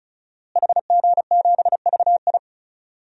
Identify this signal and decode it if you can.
Morse code, heard as a keyed tone, about 34 wpm, 700 Hz. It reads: HG7VI
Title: DXCC-100 Call Signs - 5 Characters - Repeated 3x - 35wpm